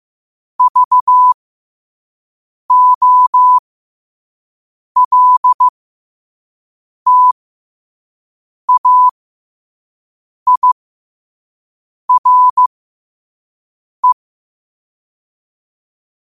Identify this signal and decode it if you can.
Morse code, heard as a keyed tone, about 15 wpm, spaced as for 5 wpm, 1000 Hz. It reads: VOLTAIRE
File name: qotdfiles / 5wpm